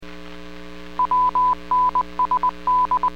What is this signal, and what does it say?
Morse code, heard as a keyed tone, about 20 wpm, 1000 Hz. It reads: WNSD